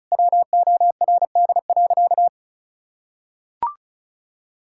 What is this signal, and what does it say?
Morse code, heard as a keyed tone, about 34 wpm, 700 Hz. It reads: WORD.
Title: Sentences from Top 1000 Words - Review Part 6 - 35wpm